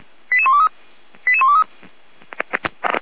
Five-tone selective call (ZVEI), heard as quick runs of five tones, tones of about 70 ms. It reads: 88123 88123